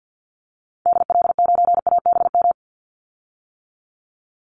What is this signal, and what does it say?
Morse code, heard as a keyed tone, about 50 wpm, 700 Hz. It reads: DL9ABM
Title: Call Signs - 50wpm